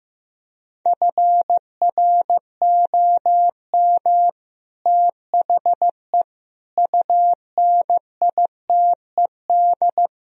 Morse code, heard as a keyed tone, about 15 wpm, 700 Hz. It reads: FROM THE UNITED
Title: Speed-Racing - Sets of 4 Words 15wpm